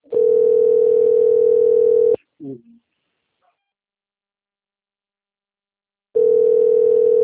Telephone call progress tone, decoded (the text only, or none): ringback tone